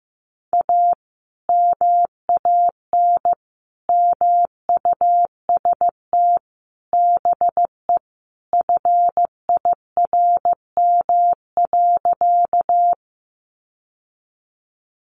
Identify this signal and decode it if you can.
Morse code, heard as a keyed tone, about 15 wpm, 700 Hz. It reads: A MAN MUST BE FIRM.
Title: Sentences from Top 1400 Words 15wpm